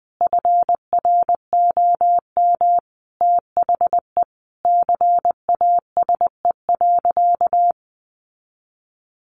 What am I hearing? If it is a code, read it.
Morse code, heard as a keyed tone, about 19 wpm, 700 Hz. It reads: FROM THE CASE.
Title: Sentences from Top 1000 Words 20wpm